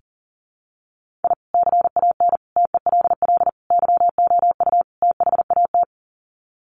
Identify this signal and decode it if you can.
Morse code, heard as a keyed tone, about 40 wpm, 700 Hz. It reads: I CAN TELL YOU THAT
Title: Sets of 5 Words 40wpm